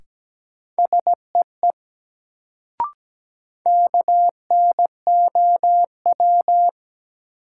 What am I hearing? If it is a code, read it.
Morse code, heard as a keyed tone, about 17 wpm, 700 Hz. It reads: SEE KNOW
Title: Sentences from Top 100 Words - New Word Drill - 17wpm